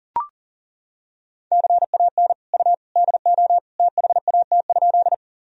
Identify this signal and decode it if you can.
Morse code, heard as a keyed tone, about 37 wpm, 700 Hz. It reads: CAN U DO THAT?
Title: Sentences from Top 100 Words and Abbreviations 40wpm